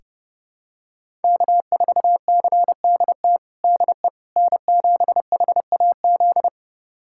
Morse code, heard as a keyed tone, about 30 wpm, 700 Hz. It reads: K4CDT DE N7HAZ
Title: USA Callsign DE Callsign - 5 Characters - 30wpm